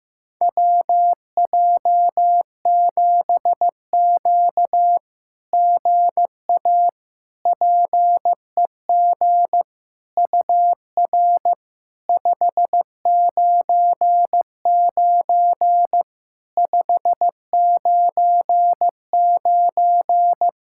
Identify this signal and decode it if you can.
Morse code, heard as a keyed tone, about 15 wpm, 700 Hz. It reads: WJ7Q GA PEG UR 599 599
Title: SOTA QSO Practice 15wpm